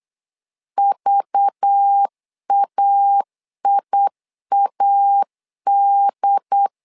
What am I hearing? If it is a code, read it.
Morse code, heard as a keyed tone, about 8 wpm, 800 Hz. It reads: VAIAD
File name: LEZIONI / 5 LBVC / 5 LBV 8wpm 2837